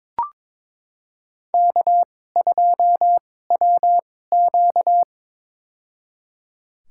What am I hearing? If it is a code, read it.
Morse code, heard as a keyed tone, about 22 wpm, spaced as for 16 wpm, 700 Hz. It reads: K2WQ
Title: USA Call Signs - 2x Character Spacing - 22wpm